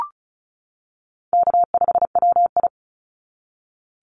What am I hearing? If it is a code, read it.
Morse code, heard as a keyed tone, about 34 wpm, 700 Hz. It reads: K5WI
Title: NA SOTA Chaser Callsigns 35wpm